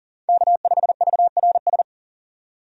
Morse code, heard as a keyed tone, about 38 wpm, 700 Hz. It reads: K5VRS
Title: USA Call Signs - 5 Characters - 40wpm